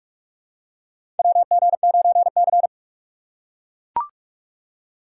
Morse code, heard as a keyed tone, about 44 wpm, 700 Hz. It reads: WG9C
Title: USA Call Signs - 4 Characters - 2x Character Spacing - 45wpm